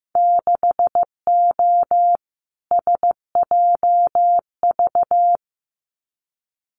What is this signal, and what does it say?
Morse code, heard as a keyed tone, about 15 wpm, 700 Hz. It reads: 6O SJV
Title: Winter Field Day Exchanges Encore 15wpm